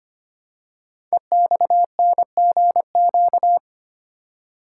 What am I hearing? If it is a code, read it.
Morse code, heard as a keyed tone, about 25 wpm, 700 Hz. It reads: EXNGQ
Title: Code Groups - 5 Letters - 25wpm